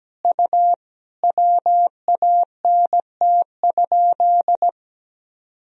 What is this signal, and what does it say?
Morse code, heard as a keyed tone, about 17 wpm, 700 Hz. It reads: U WANT?
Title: Sentences from Top 200 Words and Abbreviations 17wpm